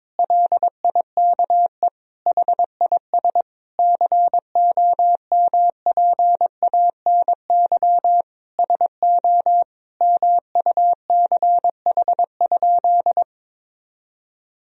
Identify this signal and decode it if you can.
Morse code, heard as a keyed tone, about 22 wpm, 700 Hz. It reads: LIKE HIS COMPANY SO MUCH?
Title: Sentences from Top 1000 Words - Review Part 3 - 22wpm